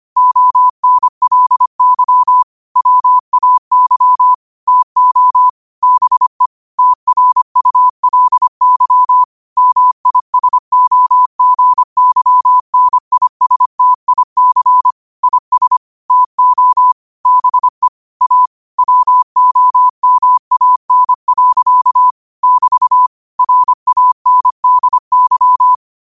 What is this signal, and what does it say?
Morse code, heard as a keyed tone, about 25 wpm, 1000 Hz. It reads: ONLY WAY TO BE TRULY MISOGYNISTIC IS TO BE A WOMAN. <BT> RANDY
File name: qotdfiles / 25wpm